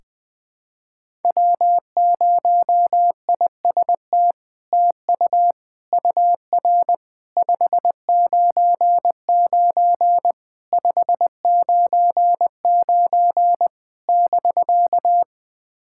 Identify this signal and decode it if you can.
Morse code, heard as a keyed tone, about 20 wpm, 700 Hz. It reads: W0IST TU UR 599 599 <BK>
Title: SOTA QSO Practice 20wpm